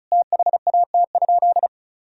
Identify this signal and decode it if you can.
Morse code, heard as a keyed tone, about 36 wpm, 700 Hz. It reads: THAT?
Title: Sentences from Top 1000 Words - Review Part 1 - 35wpm